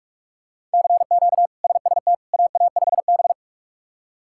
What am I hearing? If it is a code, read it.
Morse code, heard as a keyed tone, about 45 wpm, 700 Hz. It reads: CQ SST AA5B
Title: K1USN SST QSO Practice - 45wpm